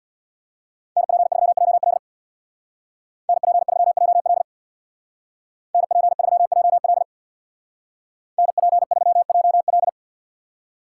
Morse code, heard as a keyed tone, about 65 wpm, 700 Hz. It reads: NP3JL NP3JL NP3JL NP3JL